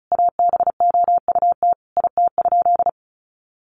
Morse code, heard as a keyed tone, about 34 wpm, 700 Hz. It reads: ABOUT IT?
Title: Sentences from Top 1100 Words 35wpm